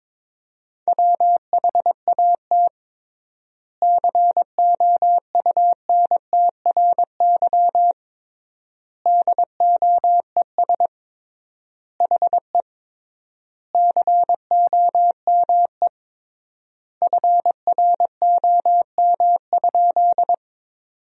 Morse code, heard as a keyed tone, about 22 wpm, 700 Hz. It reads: WHAT COUNTRY DOES HE COME FROM?
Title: 3x Word-Spacing - Sentences from Top 300 Words - 22wpm